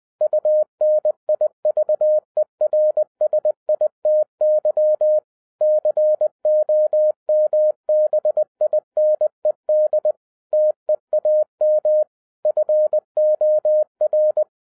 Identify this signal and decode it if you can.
Morse code, heard as a keyed tone, about 20 wpm, 610 Hz. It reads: UNIVERSITY COMBINED TEAM FOR
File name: 2011-10-mp3 / n03-TARO JR0QWW 20wpm